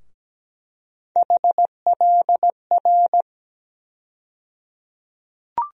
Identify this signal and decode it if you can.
Morse code, heard as a keyed tone, about 17 wpm, 700 Hz. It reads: HLR
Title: USA Call Sign Postfixes - 3 Letters 17wpm